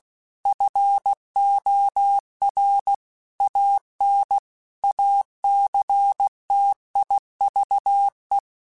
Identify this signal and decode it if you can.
Morse code, heard as a keyed tone, about 16 wpm, 800 Hz. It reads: FOR AN ACTIVE
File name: Technical Subjects / Technical Subjects-mp3s / AND Gate @15